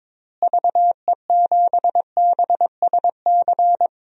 Morse code, heard as a keyed tone, about 22 wpm, 700 Hz. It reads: VE7BSC
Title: DXCC-100 Call Signs - 6 Characters - Repeated 2x - 22wpm